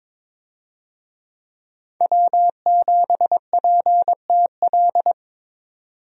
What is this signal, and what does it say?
Morse code, heard as a keyed tone, about 21 wpm, 700 Hz. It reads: W7PTL
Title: USA Call Signs - 5 Characters - 22wpm